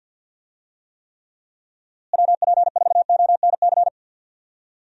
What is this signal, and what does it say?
Morse code, heard as a keyed tone, about 52 wpm, 700 Hz. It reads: WP4ONC
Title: USA Call Signs - 6 Characters - Repeated 3x - 50wpm